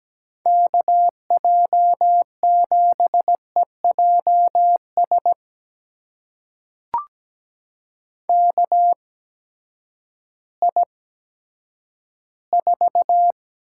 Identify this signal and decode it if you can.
Morse code, heard as a keyed tone, about 17 wpm, spaced as for 10 wpm, 700 Hz. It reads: KJ7EJS KI4
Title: USA Call Signs - 8x Character Spacing - 17wpm